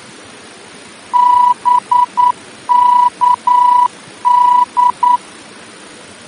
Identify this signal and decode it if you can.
Morse code, heard as a keyed tone, about 9 wpm, 980 Hz. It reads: BKD
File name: germany / mx / BKD-D-VOR